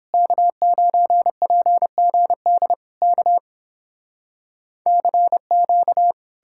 Morse code, heard as a keyed tone, about 31 wpm, 700 Hz. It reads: K9PGD K CQ